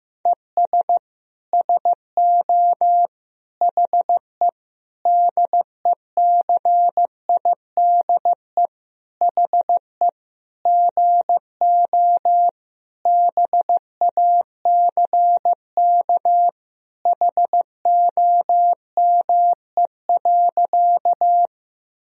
Morse code, heard as a keyed tone, about 15 wpm, 700 Hz. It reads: ES SO HE DECIDE HE GO BACK HOME.
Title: Sentences from Top 500 Words and Abbreviations 15wpm